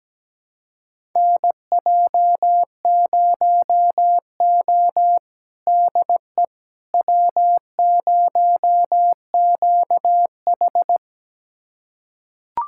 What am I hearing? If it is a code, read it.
Morse code, heard as a keyed tone, about 17 wpm, 700 Hz. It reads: NJ0O DE W0QH
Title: USA Callsign DE Callsign - 4 Characters - 17wpm